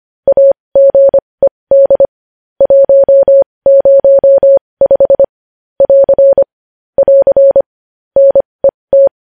Morse code, heard as a keyed tone, about 25 wpm, spaced as for 20 wpm, 550 Hz. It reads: AGED 105 <AR> <AR> NET
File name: podcast / News headlines-1x-25x20